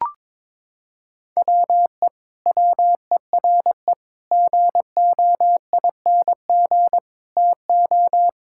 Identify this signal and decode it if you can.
Morse code, heard as a keyed tone, about 22 wpm, 700 Hz. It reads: WE WERE GOING TO
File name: practice / 22wpm / Sentences with 8-Word Phrases - 22wpm